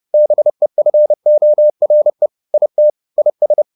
Morse code, heard as a keyed tone, about 30 wpm, 600 Hz. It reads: BEFORE IT IS
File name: the old man and the sea / the old man and the sea 1 30wpm